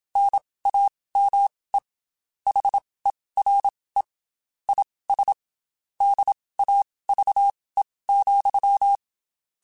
Morse code, heard as a keyed tone, about 27 wpm, spaced as for 19 wpm, 800 Hz. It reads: NAME HERE IS DAVE,
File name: QSOs / QSOs-mp3s / WD0FBT 1@25